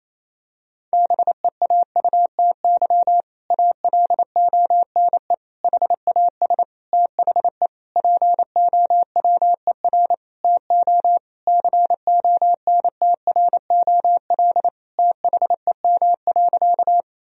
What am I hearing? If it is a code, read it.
Morse code, heard as a keyed tone, about 27 wpm, 700 Hz. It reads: BEAUTY ALONE HAS THE POWER TO CONTROL THEM.
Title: Sentences from Top 1000 Words - Review Part 2 - 28wpm